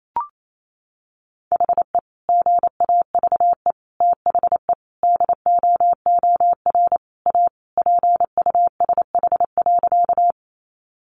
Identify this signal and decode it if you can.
Morse code, heard as a keyed tone, about 28 wpm, 700 Hz. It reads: HE GAVE THE DOOR A PUSH.
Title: Sentences from Top 1300 Words 28wpm